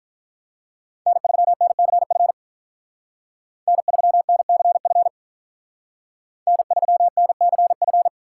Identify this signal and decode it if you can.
Morse code, heard as a keyed tone, about 45 wpm, 700 Hz. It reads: N3NCF N3NCF N3NCF